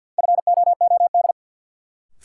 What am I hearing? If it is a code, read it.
Morse code, heard as a keyed tone, about 49 wpm, 700 Hz. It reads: FOOD
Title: Sets of 5 Words - New Word Drill - 50wpm